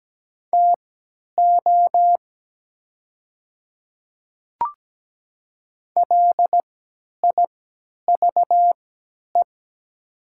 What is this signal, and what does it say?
Morse code, heard as a keyed tone, about 17 wpm, spaced as for 10 wpm, 700 Hz. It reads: TO LIVE
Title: 3x Character Spacing - Top 200 Words - 17wpm